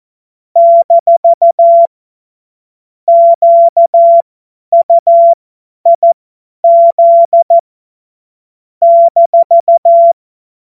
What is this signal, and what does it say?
Morse code, heard as a keyed tone, about 14 wpm, spaced as for 10 wpm, 680 Hz. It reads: - QUIZ -
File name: Morsekurs / Witze / Witz 25. Oktober 2022 50 F70